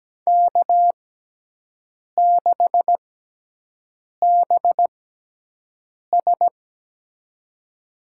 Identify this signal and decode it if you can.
Morse code, heard as a keyed tone, about 17 wpm, spaced as for 6 wpm, 700 Hz. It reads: K6BS
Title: USA Call Signs - 6x Character Spacing - 17wpm